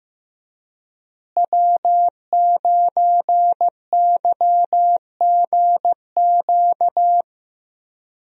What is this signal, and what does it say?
Morse code, heard as a keyed tone, about 15 wpm, 700 Hz. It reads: W9YGQ
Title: USA Call Signs - 5 Characters - Repeated 3x - 15wpm